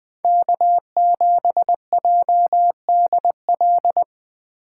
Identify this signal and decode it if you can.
Morse code, heard as a keyed tone, about 20 wpm, 700 Hz. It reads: K7JDL
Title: Speed Racing - USA 1x3 Call Signs 17wpm